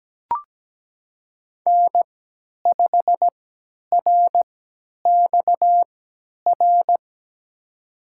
Morse code, heard as a keyed tone, about 17 wpm, spaced as for 10 wpm, 700 Hz. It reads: N5RXR
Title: USA Call Signs - 5 Characters - 3x Character Spacing - 17wpm